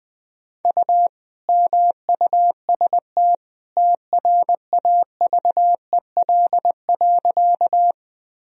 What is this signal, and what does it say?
Morse code, heard as a keyed tone, about 20 wpm, 700 Hz. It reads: U MUST TRAVEL.
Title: Sentences from Top 400 Words and Abbreviations 20wpm